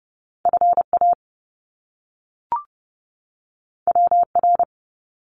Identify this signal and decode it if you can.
Morse code, heard as a keyed tone, about 30 wpm, 700 Hz. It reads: FA WR
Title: Top Two Letter Combinations in Words 30wpm